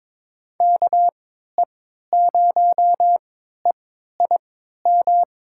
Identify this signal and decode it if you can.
Morse code, heard as a keyed tone, about 22 wpm, spaced as for 13 wpm, 700 Hz. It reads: KE0EIM
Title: USA Call Signs - 6 Characters - 3x Character Spacing - 22wpm